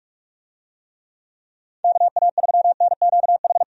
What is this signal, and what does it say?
Morse code, heard as a keyed tone, about 41 wpm, 700 Hz. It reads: KA3NQH
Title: USA Call Signs - 6 Characters - 45wpm